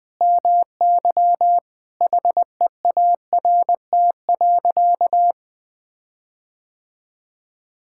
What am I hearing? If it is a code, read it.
Morse code, heard as a keyed tone, about 20 wpm, 700 Hz. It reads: MY HEART.
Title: Sentences with 9-Word Phrases - 20wpm